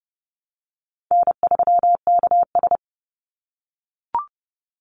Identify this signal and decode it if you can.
Morse code, heard as a keyed tone, about 30 wpm, 700 Hz. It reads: N3KS
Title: Easy Call Signs - 30wpm